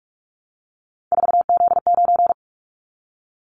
Morse code, heard as a keyed tone, about 46 wpm, 700 Hz. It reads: 4Z9
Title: Callsign Prefixes for Top 50 DXCC - 45wpm